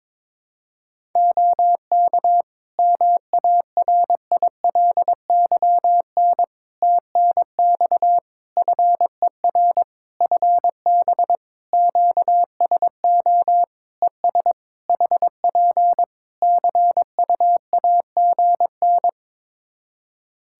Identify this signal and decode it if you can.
Morse code, heard as a keyed tone, about 22 wpm, 700 Hz. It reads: OK MARILYN TNX FER FB QSO ES HP CUAGN
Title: Practice QSO - Exchange 4 - Ending - 22wpm